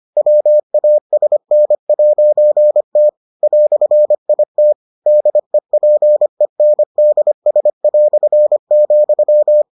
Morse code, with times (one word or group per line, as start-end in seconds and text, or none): 0.2-3.1 WASN'T
3.4-4.7 "IT
5.1-9.6 DEPENDS",